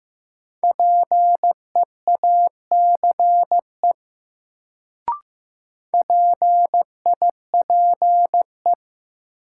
Morse code, heard as a keyed tone, about 16 wpm, 700 Hz. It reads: PEACE PIPE
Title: Sentences from Top 1500 Words - New Word Drill Alt - 15wpm